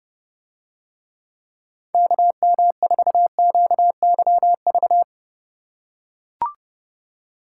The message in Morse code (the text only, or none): KM4QYV